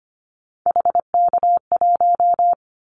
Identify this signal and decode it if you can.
Morse code, heard as a keyed tone, about 25 wpm, 700 Hz. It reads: HK1
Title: Callsign Prefixes for Top 100 DXCC - 25wpm